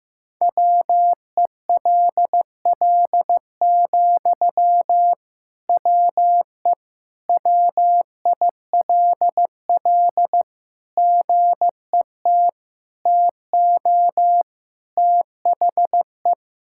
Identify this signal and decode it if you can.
Morse code, heard as a keyed tone, about 15 wpm, 700 Hz. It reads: WELL, WE WILL GET TO THE
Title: Sentences from Top 1000 Words - Review Part 6 - 15wpm